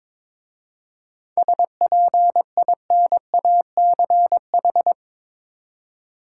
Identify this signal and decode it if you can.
Morse code, heard as a keyed tone, about 22 wpm, 690 Hz. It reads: SPINACH
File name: Difficult Words with Hints - 22wpm